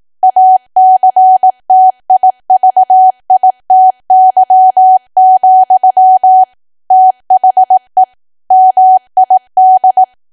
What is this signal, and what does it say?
Morse code, heard as a keyed tone, about 18 wpm, 750 Hz. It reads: ACTIVITY, THE MID
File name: cwa / W1AW-Level2 / 18WPM